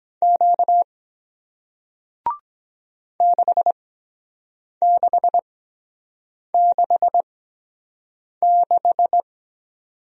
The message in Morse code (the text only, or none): Q 6666